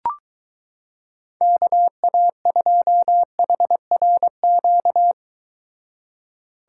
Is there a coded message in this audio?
Yes: Morse code from 1.4 to 5.1 s, heard as a keyed tone, about 23 wpm, 700 Hz. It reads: KA2HRQ